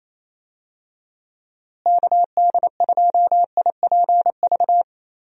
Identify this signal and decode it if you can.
Morse code, heard as a keyed tone, about 28 wpm, 700 Hz. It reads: KD2IPV
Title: USA Call Signs - 6 Characters - Repeated 3x - 28wpm